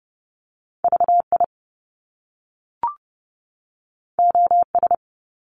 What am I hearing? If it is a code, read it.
Morse code, heard as a keyed tone, about 30 wpm, 700 Hz. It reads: VI OS
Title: Top Two Letter Combinations in Words 30wpm